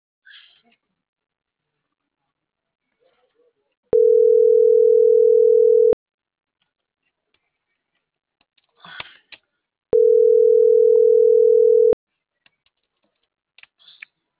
Ringback tone (North America), 440 + 480 Hz.